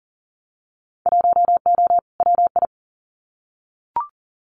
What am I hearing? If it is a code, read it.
Morse code, heard as a keyed tone, about 40 wpm, 700 Hz. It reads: 1O WI
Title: Winter Field Day Exchanges 40wpm